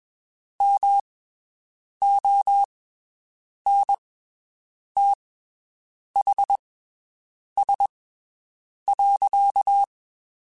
Morse code, heard as a keyed tone, about 21 wpm, spaced as for 7 wpm, 800 Hz. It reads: MONTHS.
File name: (spaced) / QSOs Spaced-mp3s / WD0BGP 1 @20 sp